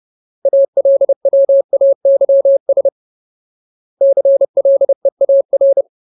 Morse code, heard as a keyed tone, about 30 wpm, 550 Hz. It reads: ALWAYS CLEAR
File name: Ultimate Word List / 30WPM Ultimate